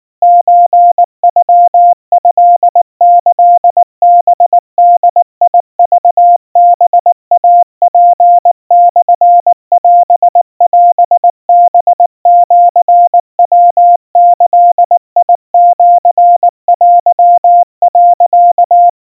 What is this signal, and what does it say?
Morse code (Japanese Wabun code), heard as a keyed tone, about 19 wpm, 700 Hz. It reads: ソノトキハボクハイツモオオハシヤギシテ、
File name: kawa / Morse sample / kamyu 70 cpm 700 Hz Crispy